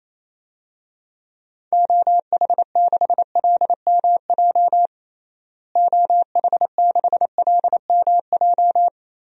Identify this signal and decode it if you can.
Morse code, heard as a keyed tone, about 28 wpm, 700 Hz. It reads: OH6LMJ OH6LMJ
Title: DXCC-100 Call Signs - 6 Characters - Repeated 3x - 28wpm